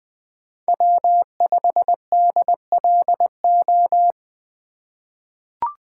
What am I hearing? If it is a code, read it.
Morse code, heard as a keyed tone, about 20 wpm, 700 Hz. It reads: W5DLO
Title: USA Call Signs - 3x Character Spacing - 20wpm